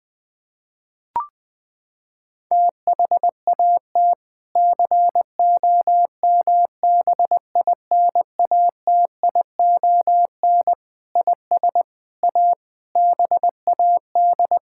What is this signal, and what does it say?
Morse code, heard as a keyed tone, about 21 wpm, 700 Hz. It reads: THAT COMBINATION IS A BAD
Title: Sentences from Top 1000 Words - Review Part 6 - 20wpm